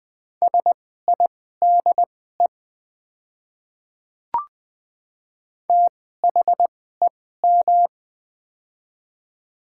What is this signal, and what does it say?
Morse code, heard as a keyed tone, about 20 wpm, spaced as for 14 wpm, 700 Hz. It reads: SIDE THEM
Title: 2x Character Spacing - Top 100 Words - 20wpm